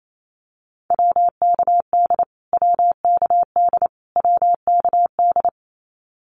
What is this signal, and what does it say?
Morse code, heard as a keyed tone, about 28 wpm, 700 Hz. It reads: WKD WKD WKD
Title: Abbreviations Repeated 3x - 28wpm